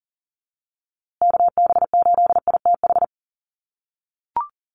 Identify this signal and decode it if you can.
Morse code, heard as a keyed tone, about 40 wpm, 700 Hz. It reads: KB8ITH
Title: USA Call Signs Repeated 2x 40wpm